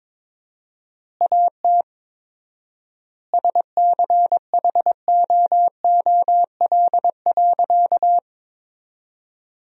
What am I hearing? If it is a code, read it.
Morse code, heard as a keyed tone, about 22 wpm, 700 Hz. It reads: AT SCHOOL.
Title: 4x Word-Spacing - Sentences from Top 300 Words - 22wpm